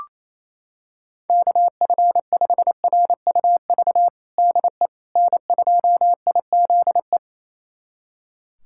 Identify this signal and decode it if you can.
Morse code, heard as a keyed tone, about 28 wpm, 700 Hz. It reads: KF5RUV DE N2IZE
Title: USA Callsign DE Callsign 28wpm